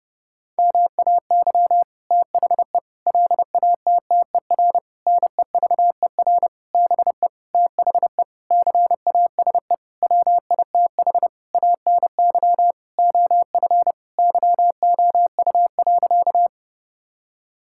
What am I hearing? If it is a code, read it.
Morse code, heard as a keyed tone, about 30 wpm, 700 Hz. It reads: MAY THE LATTER NEVER BE THE CASE WITH ANY OF YOU.
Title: Sentences from Top 1600 Words 30wpm